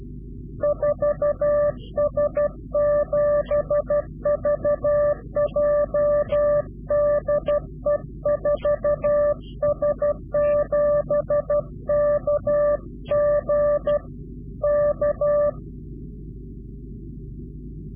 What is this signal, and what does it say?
Morse code, heard as a keyed tone, about 12 wpm, 590 Hz. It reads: 4S7VJ DE 4S7KG K
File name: CW-QSO / Directional Call / Directional QSO